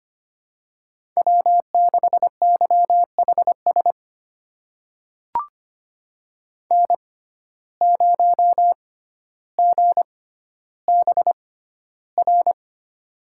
Morse code, heard as a keyed tone, about 25 wpm, spaced as for 13 wpm, 700 Hz. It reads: W6YHS N0GBR
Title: USA Call Signs - 5 Characters - 6x Character Spacing - 25wpm